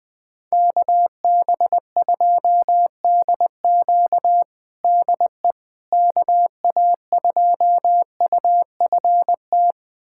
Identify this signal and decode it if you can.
Morse code, heard as a keyed tone, about 20 wpm, 700 Hz. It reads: KB2DQ DE KA2UFT
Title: USA Callsign DE Callsign 20wpm